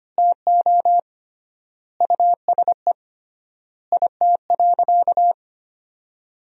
Morse code, heard as a keyed tone, about 25 wpm, 700 Hz. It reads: TO USE IT.
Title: 3x Word-Spacing - Sentences from Top 500 Words - 25wpm